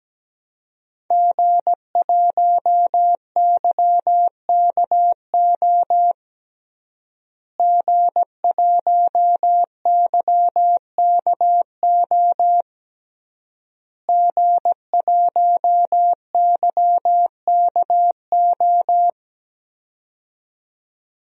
Morse code, heard as a keyed tone, about 17 wpm, 700 Hz. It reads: G1YKO G1YKO G1YKO
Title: DXCC-100 Call Signs - 5 Characters - Repeated 3x - 17wpm